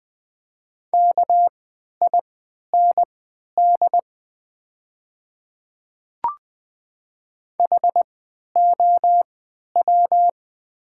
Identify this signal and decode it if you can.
Morse code, heard as a keyed tone, about 20 wpm, spaced as for 11 wpm, 700 Hz. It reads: KIND HOW
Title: 3x Character Spacing - Top 200 Words - 20wpm